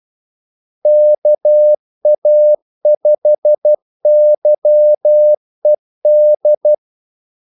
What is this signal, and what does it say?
Morse code, heard as a keyed tone, about 12 wpm, 600 Hz. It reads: KA5YED
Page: 2 ekstra ordmellemrum mellem kaldesignalerne